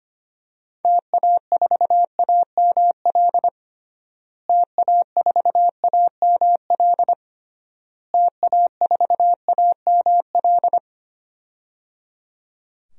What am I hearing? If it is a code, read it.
Morse code, heard as a keyed tone, about 25 wpm, 700 Hz. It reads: TA4AML TA4AML TA4AML
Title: DXCC-100 Call Signs - 6 Characters - Repeated 3x - 25wpm